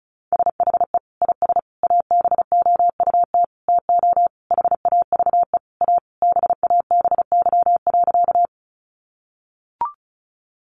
Morse code, heard as a keyed tone, about 34 wpm, 700 Hz. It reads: SHE IS ABOUT TO HAVE A BABY.
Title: Sentences from Top 1300 Words 35wpm